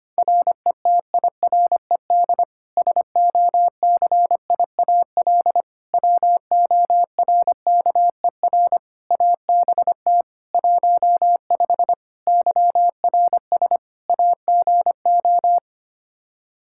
Morse code, heard as a keyed tone, about 25 wpm, 700 Hz. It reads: RETIRED SOCIAL WORKER ABT 15 YRS AGO
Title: Practice QSO - Exchange 3 - More Info - 25wpm